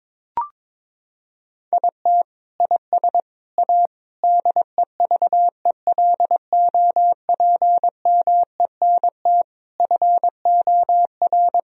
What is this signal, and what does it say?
Morse code, heard as a keyed tone, about 22 wpm, 700 Hz. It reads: IT IS A DEVELOPMENT FOR
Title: Sentences from Top 1000 Words - Review Part 4 - 22wpm